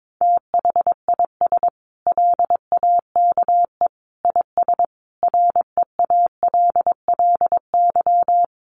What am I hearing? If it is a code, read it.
Morse code, heard as a keyed tone, about 22 wpm, 700 Hz. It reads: THIS LAKE IS REALLY
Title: Sentences from Top 1100 Words 22wpm